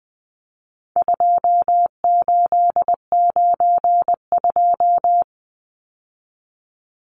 Morse code, heard as a keyed tone, about 20 wpm, 700 Hz. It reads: 2892
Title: Numbers - 4-Digit 20wpm